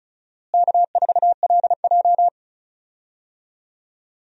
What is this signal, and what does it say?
Morse code, heard as a keyed tone, about 36 wpm, 700 Hz. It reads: K4LJ
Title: USA Call Signs - 4 Characters - Repeated 3x - 35wpm